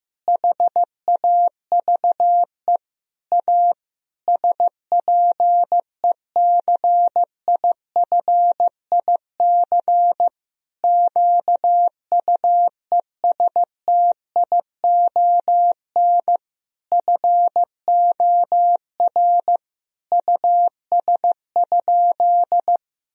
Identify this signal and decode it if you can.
Morse code, heard as a keyed tone, about 15 wpm, 700 Hz. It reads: HAVE A SPECIFIC QUESTION FOR US?
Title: Sentences from Top 1000 Words - Review Part 3 - 15wpm